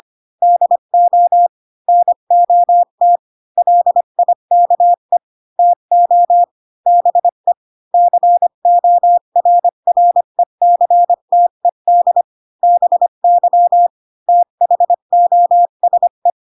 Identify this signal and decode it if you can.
Morse code, heard as a keyed tone, about 25 wpm, 700 Hz. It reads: DO NOT LIKE TO BE CORRECTED BY THOSE
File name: Pinocchio / Pinocchio ch4 20 25 ik1hge